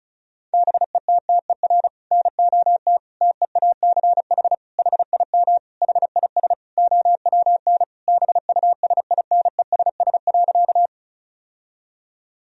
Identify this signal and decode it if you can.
Morse code, heard as a keyed tone, about 36 wpm, 700 Hz. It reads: BETTER NOT TEACH HIM HIS OWN BUSINESS.
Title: Sentences from Top 1000 Words - Review Part 2 - 35wpm